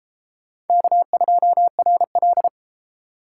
Morse code, heard as a keyed tone, about 33 wpm, 700 Hz. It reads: K2RL